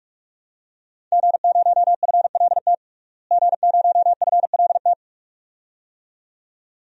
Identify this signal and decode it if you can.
Morse code, heard as a keyed tone, about 41 wpm, 700 Hz. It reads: G0FLT G0FLT
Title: DXCC-100 Call Signs - 5 Characters - Repeated 2x - 45wpm